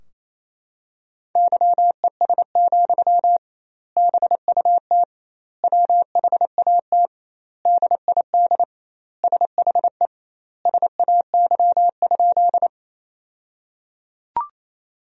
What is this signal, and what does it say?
Morse code, heard as a keyed tone, about 28 wpm, 700 Hz. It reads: YES, BUT WHAT DID SHE SAY?
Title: 2x Word-Spacing - Sentences from Top 500 Words - 28wpm